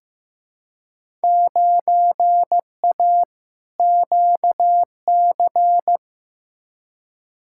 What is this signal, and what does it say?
Morse code, heard as a keyed tone, about 15 wpm, 700 Hz. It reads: 9A QC
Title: Field Day Exchanges Encore - 15wpm